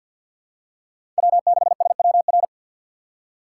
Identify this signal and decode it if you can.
Morse code, heard as a keyed tone, about 53 wpm, 700 Hz. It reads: W6SWR
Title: USA Call Signs - 4x Character Spacing - 50wpm